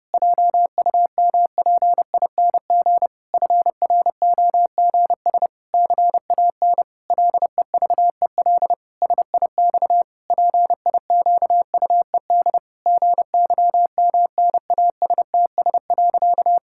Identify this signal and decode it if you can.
Morse code, heard as a keyed tone, about 30 wpm, 700 Hz. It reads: JUMPING FROGS CAN LEVEL SIX PIQUED GYMNASTS.
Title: Pangram Challenge v2 - 30wpm